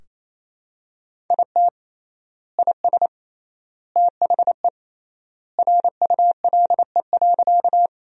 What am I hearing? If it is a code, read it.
Morse code, heard as a keyed tone, about 28 wpm, 700 Hz. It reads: IT IS THE RULE.
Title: 3x Word-Spacing - Sentences from Top 400 Words - 28wpm